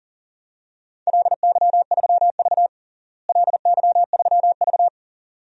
Morse code, heard as a keyed tone, about 40 wpm, 700 Hz. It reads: LY3V LY3V
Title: DXCC-100 Call Signs - 4 Characters - Repeated 2x - 40wpm